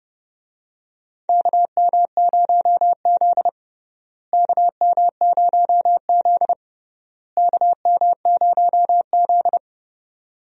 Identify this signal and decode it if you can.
Morse code, heard as a keyed tone, about 32 wpm, 700 Hz. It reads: KM0Z KM0Z KM0Z
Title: USA Call Signs - 4 Characters - Repeated 3x - 30wpm